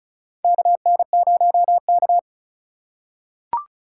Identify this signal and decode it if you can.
Morse code, heard as a keyed tone, about 36 wpm, 700 Hz. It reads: KN0K
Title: USA Call Signs - 4 Characters - 4x Character Spacing - 35wpm